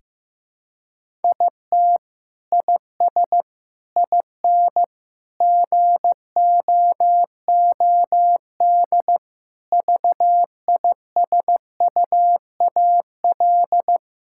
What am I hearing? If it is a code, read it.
Morse code, heard as a keyed tone, about 15 wpm, 700 Hz. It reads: IT IS IN GOOD VISUAL
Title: Sentences from Top 1000 Words - Review Part 6 - 15wpm